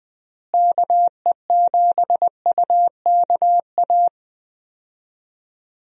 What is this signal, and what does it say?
Morse code, heard as a keyed tone, about 20 wpm, 700 Hz. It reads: KE7UKA
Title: USA Call Signs - 6 Characters - Repeated 2x - 20wpm